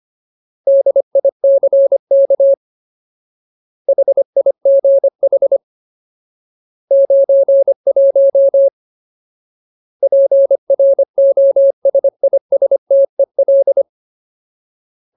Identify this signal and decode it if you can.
Morse code, heard as a keyed tone, about 25 wpm, 550 Hz. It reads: DICK HIGH 91 PROSISTEL
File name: Ultimate Word List / 25WPM Ultimate